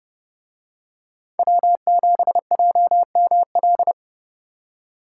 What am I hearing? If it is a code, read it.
Morse code, heard as a keyed tone, about 30 wpm, 700 Hz. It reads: W7JML